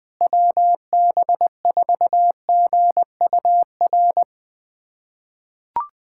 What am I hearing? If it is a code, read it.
Morse code, heard as a keyed tone, about 21 wpm, 700 Hz. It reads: WB4GUR
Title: USA Callsign DE Callsign 20wpm